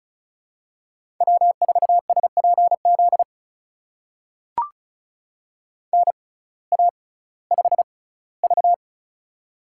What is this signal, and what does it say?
Morse code, heard as a keyed tone, about 36 wpm, spaced as for 19 wpm, 700 Hz. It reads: W4SPZ NA5V
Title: USA Call Signs - 6x Character Spacing - 35wpm